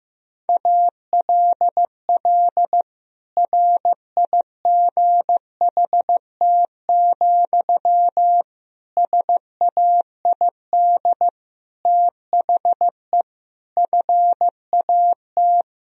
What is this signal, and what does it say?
Morse code, heard as a keyed tone, about 15 wpm, 700 Hz. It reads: ALL RIGHT, SAID THE FAT
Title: Sentences from Top 1000 Words - Review Part 4 - 15wpm